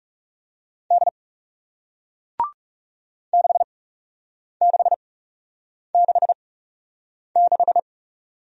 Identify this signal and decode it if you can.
Morse code, heard as a keyed tone, about 38 wpm, spaced as for 8 wpm, 700 Hz. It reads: D 6666